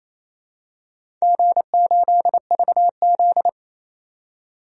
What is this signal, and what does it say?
Morse code, heard as a keyed tone, about 28 wpm, 700 Hz. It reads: G8VZ
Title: DXCC-100 Call Signs - 4 Characters - 28wpm